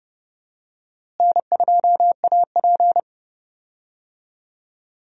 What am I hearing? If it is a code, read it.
Morse code, heard as a keyed tone, about 30 wpm, 700 Hz. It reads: N2AP
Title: USA Call Signs - 4 Characters - Repeated 3x - 30wpm